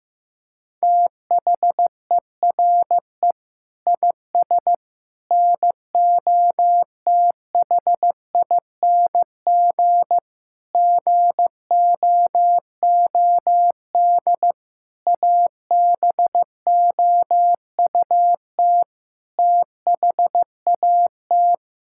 Morse code, heard as a keyed tone, about 15 wpm, 700 Hz. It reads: THERE IS NOTHING GOOD ABOUT THAT
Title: Sentences from Top 1000 Words - Review Part 5 - 15wpm